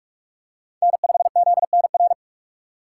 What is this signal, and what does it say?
Morse code, heard as a keyed tone, about 46 wpm, 700 Hz. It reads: N5ZNR
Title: USA Call Signs - 5 Characters - 4x Character Spacing - 45wpm